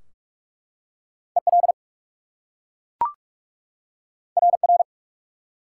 Morse code, heard as a keyed tone, about 44 wpm, 700 Hz. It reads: EL RR
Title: Top 180 QSO Elements 45wpm